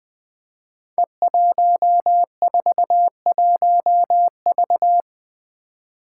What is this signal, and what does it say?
Morse code, heard as a keyed tone, about 20 wpm, 700 Hz. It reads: E141V
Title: Five Challenging Letter-Number Combinations 20wpm